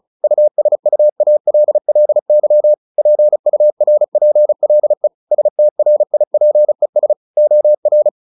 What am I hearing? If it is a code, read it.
Morse code, heard as a keyed tone, about 34 wpm, 600 Hz. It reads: USUALLY PURPLE STRIPES OR
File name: the old man and the sea 2 35wpm